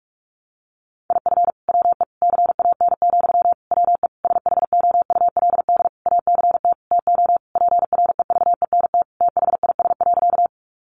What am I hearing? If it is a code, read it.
Morse code, heard as a keyed tone, about 44 wpm, 700 Hz. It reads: IF WE CAN, WE SHOULD ACT TO PREVENT THIS.
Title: Sentences from Top 1100 Words 45wpm